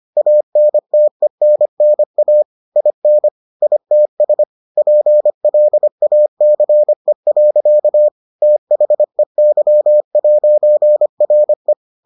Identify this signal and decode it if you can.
Morse code, heard as a keyed tone, about 25 wpm, 600 Hz. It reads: ANTENNA IN ITS PLACE. THEY'RE